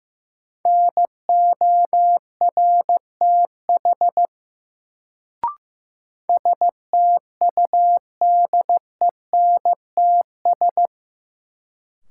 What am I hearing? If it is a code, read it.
Morse code, heard as a keyed tone, about 15 wpm, 700 Hz. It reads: NORTH STUDENTS
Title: Sets of 5 Words Encore - New Word Drill - 15wpm